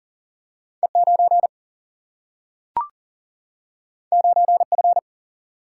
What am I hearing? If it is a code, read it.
Morse code, heard as a keyed tone, about 40 wpm, 700 Hz. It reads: E9 9F
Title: Single-Letter Single-Number 40wpm